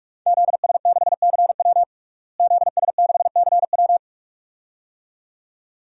Morse code, heard as a keyed tone, about 46 wpm, 700 Hz. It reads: ZS6CW ZS6CW
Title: DXCC-100 Call Signs - 5 Characters - Repeated 2x - 45wpm